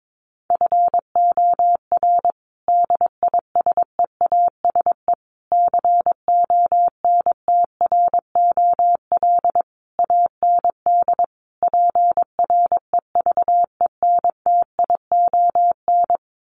Morse code, heard as a keyed tone, about 22 wpm, 700 Hz. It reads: FOR DISEASE CONTROL AND PREVENTION
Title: Sets of 5 Words Encore2 22wpm